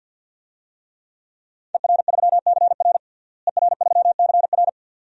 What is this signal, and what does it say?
Morse code, heard as a keyed tone, about 51 wpm, 700 Hz. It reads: ER3CR ER3CR
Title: DXCC-100 Call Signs - 5 Characters - Repeated 2x - 50wpm